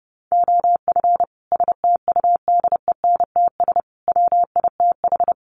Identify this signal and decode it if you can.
Morse code, heard as a keyed tone, about 30 wpm, 700 Hz. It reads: OF STUDENTS WITH
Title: Sets of 5 Words Encore 30wpm